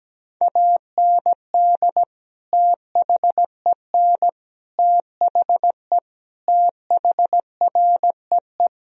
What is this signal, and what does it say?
Morse code, heard as a keyed tone, about 17 wpm, 700 Hz. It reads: AND THEN THE THREE
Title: Sentences from Top 1000 Words - Review Part 1 - 17wpm